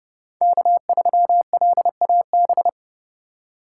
Morse code, heard as a keyed tone, about 30 wpm, 700 Hz. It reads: K3LAB
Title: USA Call Signs - 5 Characters - 2x Character Spacing - 30wpm